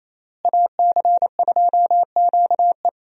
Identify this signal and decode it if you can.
Morse code, heard as a keyed tone, about 28 wpm, 700 Hz. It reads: AC2QE
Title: USA Call Signs - 5 Characters - Repeated 2x - 28wpm